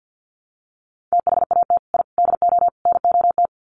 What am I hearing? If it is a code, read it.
Morse code, heard as a keyed tone, about 50 wpm, 700 Hz. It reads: THAT I DO NOT
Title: Sets of 4 Words 50wpm